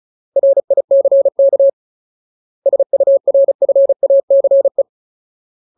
Morse code, heard as a keyed tone, about 35 wpm, 550 Hz. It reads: RICK SURFACE
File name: Ultimate Word List / 35WPM Ultimate